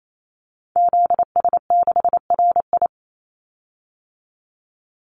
Morse code, heard as a keyed tone, about 28 wpm, 700 Hz. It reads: ZS6RI
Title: Call Signs - 28wpm